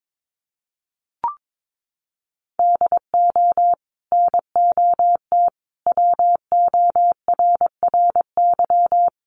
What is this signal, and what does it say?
Morse code, heard as a keyed tone, about 22 wpm, 700 Hz. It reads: DO NOT WORRY
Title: Sets of 5 Words 22wpm